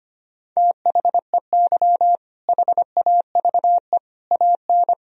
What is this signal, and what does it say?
Morse code, heard as a keyed tone, about 25 wpm, 700 Hz. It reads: THEY HAVE AN
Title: Sentences from Top 1000 Words - Review Part 4 - 25wpm